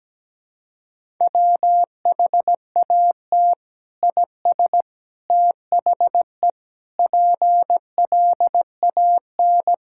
Morse code, heard as a keyed tone, about 17 wpm, 700 Hz. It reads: WHAT IS THE PLAN
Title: Sentences from Top 1000 Words - Review Part 5 - 17wpm